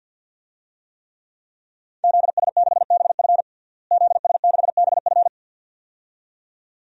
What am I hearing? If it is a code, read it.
Morse code, heard as a keyed tone, about 52 wpm, 700 Hz. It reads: ZS6BF ZS6BF
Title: DXCC-100 Call Signs - 5 Characters - Repeated 2x - 50wpm